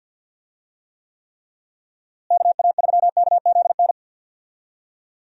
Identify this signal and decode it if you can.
Morse code, heard as a keyed tone, about 50 wpm, 700 Hz. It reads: KA3KZN
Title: USA Call Signs - 6 Characters - Repeated 3x - 50wpm